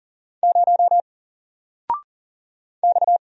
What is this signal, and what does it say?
Morse code, heard as a keyed tone, about 40 wpm, 700 Hz. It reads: 0 X
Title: Single Letter-Number Alt 40wpm